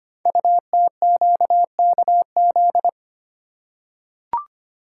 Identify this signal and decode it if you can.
Morse code, heard as a keyed tone, about 25 wpm, 700 Hz. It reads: UTQKZ
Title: Code Groups - 5 Letters - 25wpm